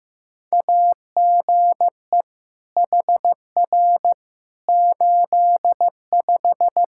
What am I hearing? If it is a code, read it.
Morse code, heard as a keyed tone, about 15 wpm, 700 Hz. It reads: AGE HR 85
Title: Practice QSO - Exchange 3 - More Info - 15wpm